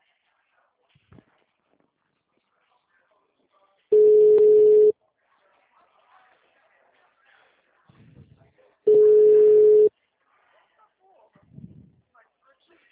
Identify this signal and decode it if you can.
Ringback tone (Europe, 425 Hz), with speech.